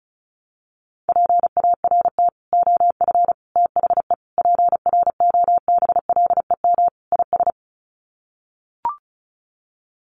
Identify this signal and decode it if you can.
Morse code, heard as a keyed tone, about 34 wpm, 700 Hz. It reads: PART OF THE PROBLEM IS
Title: Sets of 5 Words Encore2 35wpm